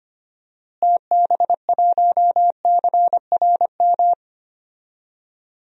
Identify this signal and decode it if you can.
Morse code, heard as a keyed tone, about 25 wpm, 700 Hz. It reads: TB1CRM
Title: DXCC-100 Call Signs - 6 Characters - Repeated 3x - 25wpm